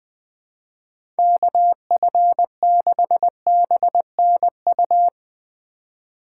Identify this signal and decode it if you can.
Morse code, heard as a keyed tone, about 20 wpm, 700 Hz. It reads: KF6BNU
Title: USA Call Signs - 3x Character Spacing - 20wpm